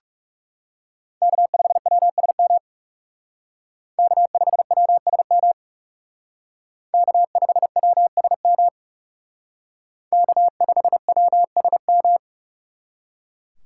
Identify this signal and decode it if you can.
Morse code, heard as a keyed tone, about 34 wpm, 700 Hz. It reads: K5WSM K5WSM K5WSM K5WSM
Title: Speed Racing - USA 1x3 Call Signs 30wpm